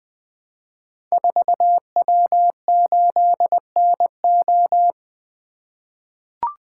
Morse code, heard as a keyed tone, about 20 wpm, 700 Hz. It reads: 4W8NO
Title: Code Groups - 5 Letter:Numbers - 20wpm